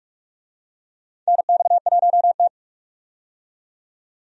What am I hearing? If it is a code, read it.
Morse code, heard as a keyed tone, about 45 wpm, 700 Hz. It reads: NX1T
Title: USA Call Signs - 4 Characters - 45wpm